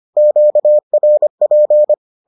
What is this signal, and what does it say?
Morse code, heard as a keyed tone, about 25 wpm, 600 Hz. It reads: QRP